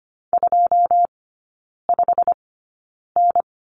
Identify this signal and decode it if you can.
Morse code, heard as a keyed tone, about 25 wpm, spaced as for 9 wpm, 690 Hz. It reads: 25N
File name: Lesson 15e - Number 5 - Three Characters 25wpm